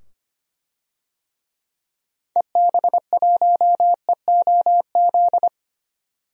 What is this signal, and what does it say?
Morse code, heard as a keyed tone, about 25 wpm, 700 Hz. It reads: EB1EOZ
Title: DXCC-100 Call Signs - 6 Characters - 25wpm